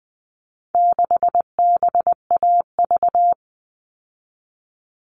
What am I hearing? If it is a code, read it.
Morse code, heard as a keyed tone, about 20 wpm, 700 Hz. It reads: 6BAV
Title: All Cut Numbers - 4-Digit 20wpm